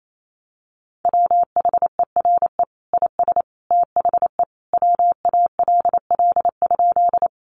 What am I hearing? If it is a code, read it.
Morse code, heard as a keyed tone, about 28 wpm, 700 Hz. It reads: WHERE IS THE WALL?
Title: Sentences from Top 1100 Words 28wpm